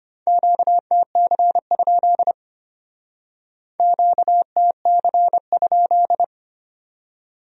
Morse code, heard as a keyed tone, about 32 wpm, 700 Hz. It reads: QTC? QTC?